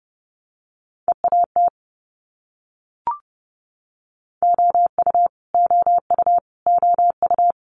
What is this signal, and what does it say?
Morse code, heard as a keyed tone, about 30 wpm, 700 Hz. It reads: EAT OU OU OU
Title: Multi-letter Phonemes Repeated 3x 30wpm